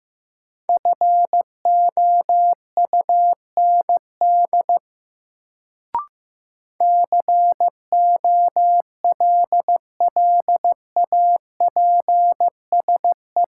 Morse code, heard as a keyed tone, about 15 wpm, 700 Hz. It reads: FOUND COLLAPSE
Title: Sets of 5 Words Encore - New Word Drill - 15wpm